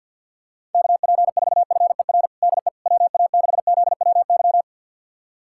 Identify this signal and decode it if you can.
Morse code, heard as a keyed tone, about 51 wpm, 700 Hz. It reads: KP4FER DE WA6ZWY
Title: USA Callsign DE Callsign - 6 Characters - 50wpm